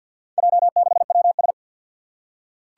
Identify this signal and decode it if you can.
Morse code, heard as a keyed tone, about 49 wpm, 700 Hz. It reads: J6WS
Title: Four Challenging Letter-Number Combinations 50wpm